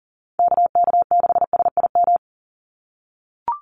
Morse code, heard as a keyed tone, about 40 wpm, 700 Hz. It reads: KK6SIM
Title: USA Call Signs 40wpm